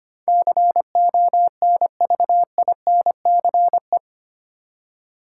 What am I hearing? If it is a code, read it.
Morse code, heard as a keyed tone, about 25 wpm, 700 Hz. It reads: CONVINCE
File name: Eight Letter Words - 25wpm